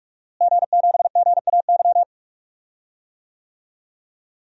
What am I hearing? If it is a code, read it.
Morse code, heard as a keyed tone, about 46 wpm, 700 Hz. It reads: G7GAY
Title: DXCC-100 Call Signs - 5 Characters - Repeated 3x - 45wpm